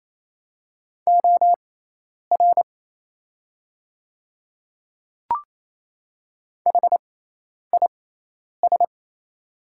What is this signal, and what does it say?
Morse code, heard as a keyed tone, about 28 wpm, spaced as for 10 wpm, 700 Hz. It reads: OR HIS
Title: 6x Character Spacing - Top 100 Words - 28wpm